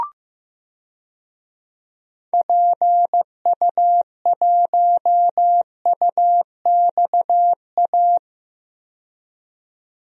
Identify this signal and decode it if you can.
Morse code, heard as a keyed tone, about 15 wpm, 700 Hz. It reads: PU1UXA
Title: DXCC-100 Call Signs - 6 Characters - 15wpm